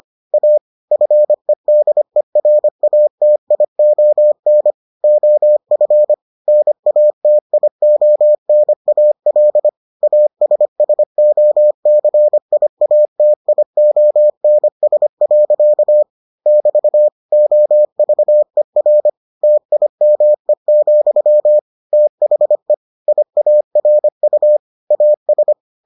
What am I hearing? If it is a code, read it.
Morse code, heard as a keyed tone, about 25 wpm, 600 Hz. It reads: A FEDERATION OF NATIONAL ASSOCIATIONS. <BT> OVER TIME, THE IARU AS